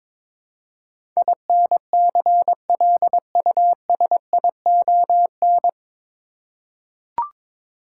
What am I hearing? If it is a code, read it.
Morse code, heard as a keyed tone, about 22 wpm, 700 Hz. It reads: INCLUSION
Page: Morse Code Ninja practice 22wpm